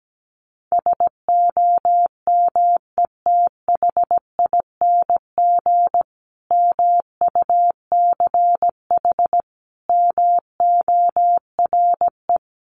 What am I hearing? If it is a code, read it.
Morse code, heard as a keyed tone, about 17 wpm, 700 Hz. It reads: SOMETHING MUCH MORE
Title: A Sound Of Thunder 17wpm